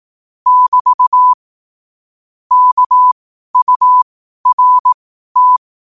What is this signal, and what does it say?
Morse code, heard as a keyed tone, about 18 wpm, spaced as for 12 wpm, 1000 Hz. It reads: <BT> KURT
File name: qotdfiles / 12wpm